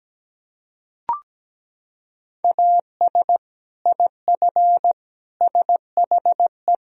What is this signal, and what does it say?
Morse code, heard as a keyed tone, about 17 wpm, 700 Hz. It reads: AS IF SHE
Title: Sentences from Top 1000 Words - Review Part 1 - 17wpm